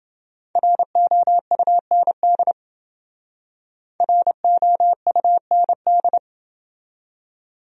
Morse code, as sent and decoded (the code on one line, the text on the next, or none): .-. --- ..- -. -.. / .-. --- ..- -. -..
ROUND ROUND